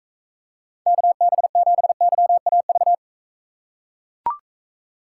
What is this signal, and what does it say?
Morse code, heard as a keyed tone, about 39 wpm, 700 Hz. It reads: KB7YAV